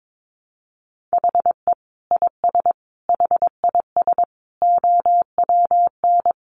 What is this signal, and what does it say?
Morse code, heard as a keyed tone, about 22 wpm, 700 Hz. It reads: HE IS HIS OWN
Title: Sentences from Top 1300 Words 22wpm